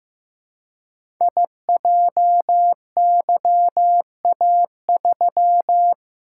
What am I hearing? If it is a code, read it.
Morse code, heard as a keyed tone, about 15 wpm, 700 Hz. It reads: IJYA3
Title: Code Groups - 5 Letter:Numbers - 15wpm